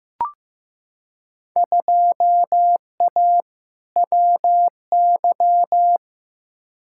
Morse code, heard as a keyed tone, about 15 wpm, 700 Hz. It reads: 2A WY
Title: Field Day Exchanges - 15wpm